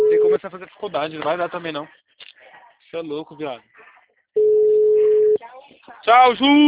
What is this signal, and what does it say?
Ringback tone (Europe, 425 Hz), with speech.